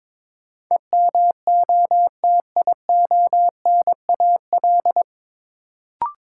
Morse code, heard as a keyed tone, about 22 wpm, 700 Hz. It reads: EMOTIONAL
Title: Sentences from Top 1000 Words - Review Part 5 - New Word Drill - 22wpm